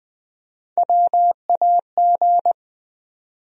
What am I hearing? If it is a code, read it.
Morse code, heard as a keyed tone, about 20 wpm, 700 Hz. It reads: WAG
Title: USA Call Sign Postfixes - 3 Letters 20wpm